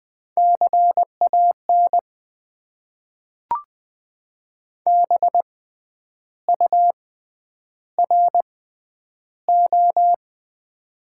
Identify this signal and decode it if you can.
Morse code, heard as a keyed tone, about 20 wpm, spaced as for 9 wpm, 700 Hz. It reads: CAN BURO
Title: Top 90 QSO Elements - 6x Character Spacing - 20wpm